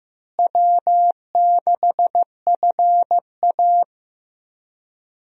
Morse code, heard as a keyed tone, about 16 wpm, 700 Hz. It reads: W6FA
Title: USA Call Signs - 4 Characters - Repeated 3x - 15wpm